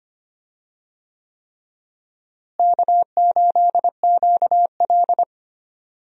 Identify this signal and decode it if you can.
Morse code, heard as a keyed tone, about 25 wpm, 700 Hz. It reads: K8QL
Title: USA Call Signs - 4 Characters - Repeated 3x - 25wpm